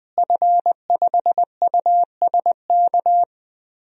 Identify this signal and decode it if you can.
Morse code, heard as a keyed tone, about 20 wpm, 700 Hz. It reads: F5USK
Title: DXCC-100 Call Signs - 5 Characters - Repeated 2x - 20wpm